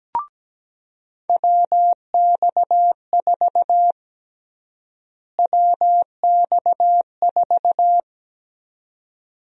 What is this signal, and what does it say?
Morse code, heard as a keyed tone, about 17 wpm, 700 Hz. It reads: WX4 WX4
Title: USA Call Sign Prefixes - Repeated 2x - 17wpm